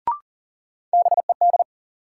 Morse code, heard as a keyed tone, about 40 wpm, 700 Hz. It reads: BED
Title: Sets of 5 Words Encore - New Word Drill - 40wpm